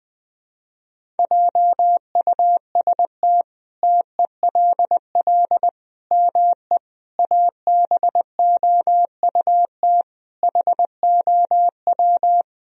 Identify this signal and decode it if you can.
Morse code, heard as a keyed tone, about 20 wpm, 700 Hz. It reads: JUST TELL ME ABOUT HOW
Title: Sentences from Top 1000 Words - Review Part 1 - 20wpm